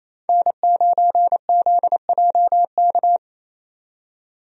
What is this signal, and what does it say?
Morse code, heard as a keyed tone, about 28 wpm, 700 Hz. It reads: N9ZJK
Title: USA Call Signs - 5 Characters - 28wpm